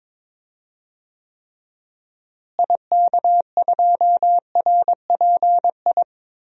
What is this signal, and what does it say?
Morse code, heard as a keyed tone, about 22 wpm, 700 Hz. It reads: IK2RPI
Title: DXCC-100 Call Signs - 6 Characters - Repeated 2x - 22wpm